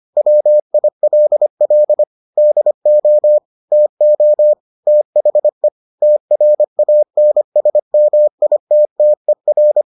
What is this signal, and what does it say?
Morse code, heard as a keyed tone, about 25 wpm, 600 Hz. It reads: WILL DO TO THE TRANSMITTER